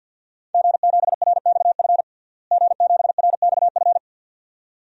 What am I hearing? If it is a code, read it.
Morse code, heard as a keyed tone, about 52 wpm, 700 Hz. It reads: G7RXF G7RXF
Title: DXCC-100 Call Signs - 5 Characters - Repeated 2x - 50wpm